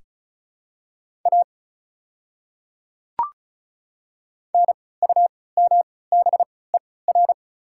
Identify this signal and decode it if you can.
Morse code, heard as a keyed tone, about 34 wpm, spaced as for 20 wpm, 700 Hz. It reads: A NUMBER
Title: 3x Character Spacing - Top 300 Words - 35wpm